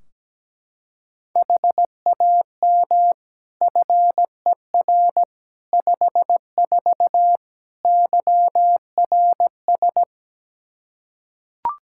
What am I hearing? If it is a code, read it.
Morse code, heard as a keyed tone, about 17 wpm, 700 Hz. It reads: HAM FER 54 YRS
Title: K1USN SST QSO Practice 17wpm